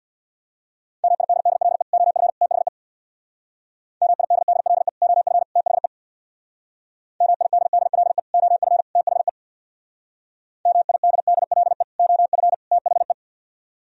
Morse code, heard as a keyed tone, about 64 wpm, 700 Hz. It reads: MIDDLE OF THE MIDDLE OF THE MIDDLE OF THE MIDDLE OF THE